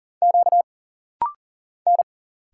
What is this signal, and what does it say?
Morse code, heard as a keyed tone, about 40 wpm, 700 Hz. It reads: Q N
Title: Single Letters - Rapid-Fire 40wpm